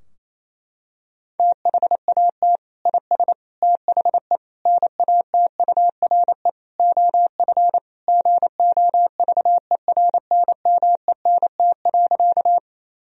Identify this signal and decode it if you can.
Morse code, heard as a keyed tone, about 28 wpm, 700 Hz. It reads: THAT IS THE NATURE OF GOVERNMENT.
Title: Sentences from Top 1000 Words - Review Part 3 - 28wpm